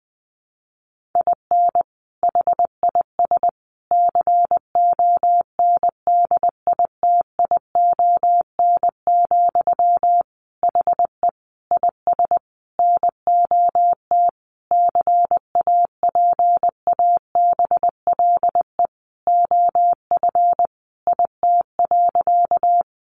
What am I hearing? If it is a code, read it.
Morse code, heard as a keyed tone, about 20 wpm, 700 Hz. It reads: IN HIS CONDITION, HE IS NOT CAPABLE OF IT.
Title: Sentences from Top 1200 Words 20wpm